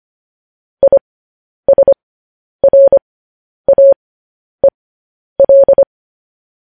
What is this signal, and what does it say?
Morse code, heard as a keyed tone, about 25 wpm, spaced as for 10 wpm, 550 Hz. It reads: ISRAEL
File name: podcast / News headlines-1x-25x10